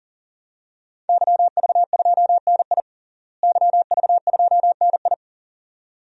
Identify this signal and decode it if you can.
Morse code, heard as a keyed tone, about 40 wpm, 700 Hz. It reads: YV2NI YV2NI
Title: DXCC-100 Call Signs - 5 Characters - Repeated 2x - 40wpm